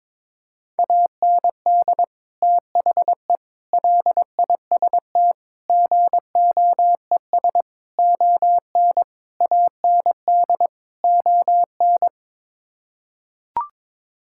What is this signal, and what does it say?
Morse code, heard as a keyed tone, about 22 wpm, 700 Hz. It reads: AND THE LIST GOES ON AND ON
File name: practice / 22wpm / Sets of 7 Words - 22wpm